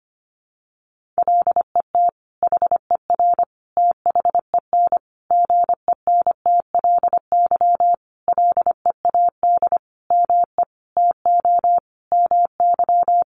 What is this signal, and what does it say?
Morse code, heard as a keyed tone, about 25 wpm, 700 Hz. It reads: LET HER THEN GENTLY LEAD ME TO MY
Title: Sentences from Top 1500 Words 25wpm